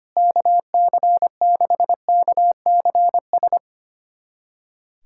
Morse code, heard as a keyed tone, about 25 wpm, 700 Hz. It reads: KC6KCS
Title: USA Callsign DE Callsign - 6 Characters - 25wpm